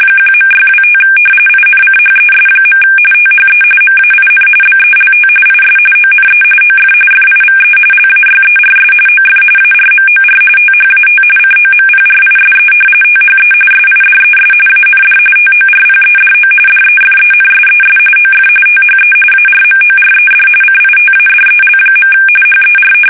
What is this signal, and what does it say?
RTTY, about 45.45 baud. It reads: 1.625 INCHES. THE MAIN BELT IS ONE OF FIVE LAYERS OF ARMOR THAT COMPRISE THE SIDES AND TORPEDO DEFENSES OF THE ARMORED CITADEL. HER
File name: fsk850 / battleshipnj